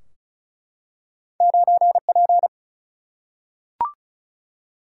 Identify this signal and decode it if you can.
Morse code, heard as a keyed tone, about 34 wpm, 700 Hz. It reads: 9 P
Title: Single-Letter Single-Number 35wpm